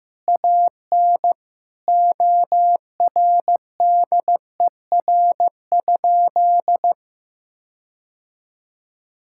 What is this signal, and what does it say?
Morse code, heard as a keyed tone, about 15 wpm, 700 Hz. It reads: AN ORDER?
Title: Sentences from Top 1600 Words 15wpm